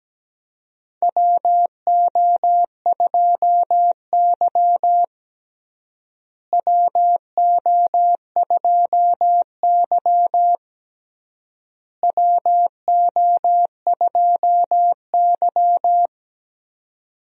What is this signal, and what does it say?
Morse code, heard as a keyed tone, about 17 wpm, 700 Hz. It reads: WO2Y WO2Y WO2Y
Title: USA Call Signs - 4 Characters - Repeated 3x - 17wpm